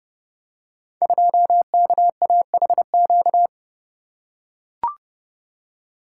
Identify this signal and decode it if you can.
Morse code, heard as a keyed tone, about 30 wpm, 700 Hz. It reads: 2KAHQ
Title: Code Groups - 5 Letter:Numbers - 30wpm